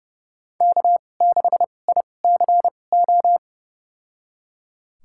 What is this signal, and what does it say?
Morse code, heard as a keyed tone, about 30 wpm, spaced as for 22 wpm, 700 Hz. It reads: K6ICO
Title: USA Call Signs - 5 Characters - 2x Character Spacing - 30wpm